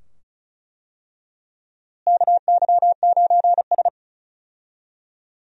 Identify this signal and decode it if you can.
Morse code, heard as a keyed tone, about 34 wpm, 700 Hz. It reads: KY9S
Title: USA Call Signs - 4 Characters - Repeated 2x - 35wpm